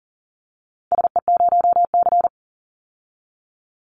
Morse code, heard as a keyed tone, about 41 wpm, 700 Hz. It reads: SE0C
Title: Call Signs - 40wpm